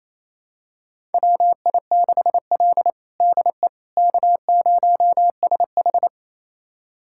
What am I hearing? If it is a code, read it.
Morse code, heard as a keyed tone, about 28 wpm, 700 Hz. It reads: WI6L DE K0SH
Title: USA Callsign DE Callsign - 4 Characters - 28wpm